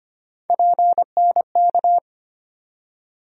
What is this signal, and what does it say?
Morse code, heard as a keyed tone, about 25 wpm, 700 Hz. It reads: PNK
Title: USA Call Sign Postfixes - 3 Letters 25wpm